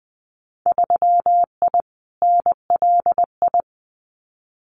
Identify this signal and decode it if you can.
Morse code, heard as a keyed tone, about 20 wpm, 700 Hz. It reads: 3I NLI
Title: Winter Field Day Exchanges Encore 20wpm